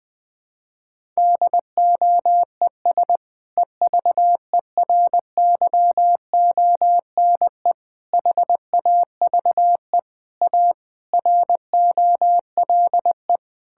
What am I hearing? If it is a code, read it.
Morse code, heard as a keyed tone, about 21 wpm, 700 Hz. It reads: DOES EVERYONE HAVE A ROLE
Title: Sentences from Top 1000 Words - Review Part 6 - 20wpm